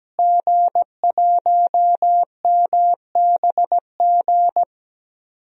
Morse code, heard as a keyed tone, about 17 wpm, 700 Hz. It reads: G1MBG
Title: DXCC-100 Call Signs - 5 Characters - Repeated 3x - 17wpm